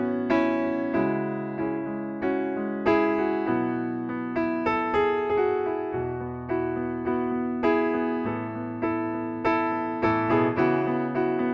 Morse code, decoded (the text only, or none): none